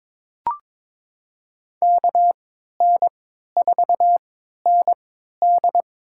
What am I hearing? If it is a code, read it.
Morse code, heard as a keyed tone, about 22 wpm, spaced as for 12 wpm, 700 Hz. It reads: KN4ND
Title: USA Call Signs - 3x Character Spacing - 22wpm